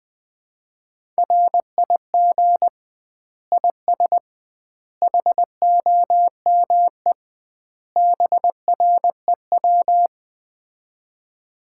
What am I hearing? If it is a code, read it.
Morse code, heard as a keyed tone, about 21 wpm, 700 Hz. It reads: RIG IS HOME BREW
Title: Common QSO Phrases - 2x Word-Spacing - 20wpm